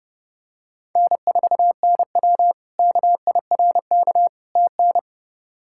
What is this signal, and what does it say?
Morse code, heard as a keyed tone, about 32 wpm, 700 Hz. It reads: N4NW KIRK TN
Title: K1USN SST QSO Practice - 30wpm